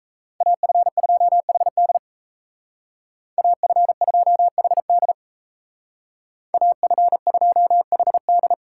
Morse code, heard as a keyed tone, about 37 wpm, 700 Hz. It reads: AF2HD AF2HD AF2HD